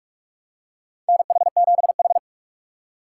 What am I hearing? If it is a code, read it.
Morse code, heard as a keyed tone, about 46 wpm, 700 Hz. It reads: NH7H
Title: USA Call Signs - 4 Characters - 6x Character Spacing - 45wpm